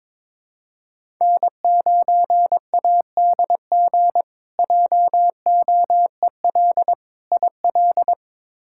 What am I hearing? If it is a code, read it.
Morse code, heard as a keyed tone, about 22 wpm, 700 Hz. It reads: N9ADG JOEL IL
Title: K1USN SST QSO Practice - 22wpm